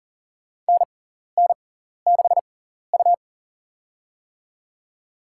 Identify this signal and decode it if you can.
Morse code, heard as a keyed tone, about 40 wpm, spaced as for 14 wpm, 700 Hz. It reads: NN6U
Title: USA Call Signs - 4 Characters - 6x Character Spacing - 40wpm